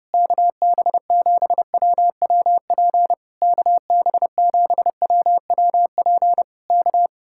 Morse code, heard as a keyed tone, about 27 wpm, 700 Hz. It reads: KB7WWP KB7WWP K
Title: Speed-Racing Common QSO Phrases - 25wpm